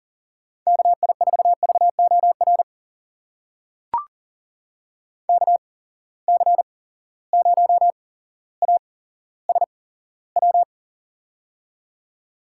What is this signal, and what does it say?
Morse code, heard as a keyed tone, about 40 wpm, spaced as for 17 wpm, 700 Hz. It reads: KI4VOR KC0ASW
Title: USA Call Signs - 6 Characters - 8x Character Spacing - 40wpm